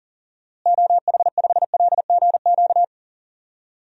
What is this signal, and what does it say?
Morse code, heard as a keyed tone, about 38 wpm, 700 Hz. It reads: OH5LGQ
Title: DXCC-100 Call Signs - 6 Characters - Repeated 2x - 40wpm